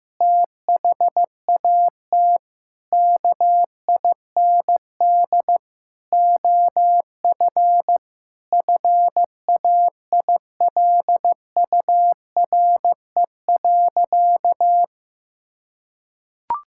Morse code, heard as a keyed tone, about 16 wpm, 700 Hz. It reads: THAT KIND OF FAILURE.
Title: Sentences from Top 1000 Words - Review Part 6 - 15wpm